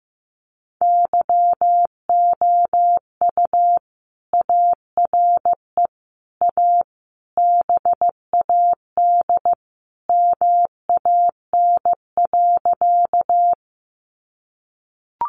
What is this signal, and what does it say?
Morse code, heard as a keyed tone, about 15 wpm, 700 Hz. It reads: YOU ARE A BAD MAN.
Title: Sentences from Top 700 Words 15wpm